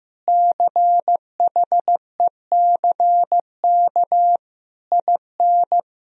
Morse code, heard as a keyed tone, about 15 wpm, 700 Hz. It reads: CHECK IN
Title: Sentences from Top 1000 Words - Review Part 3 - 15wpm